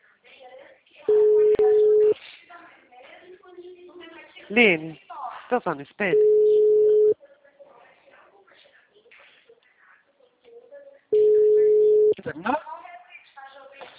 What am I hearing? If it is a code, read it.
Ringback tone (Europe, 425 Hz), with speech.